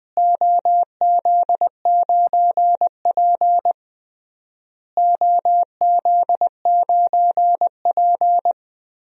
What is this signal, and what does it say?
Morse code, heard as a keyed tone, about 20 wpm, 700 Hz. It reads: OZ9P OZ9P
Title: DXCC-100 Call Signs - 4 Characters - Repeated 3x - 20wpm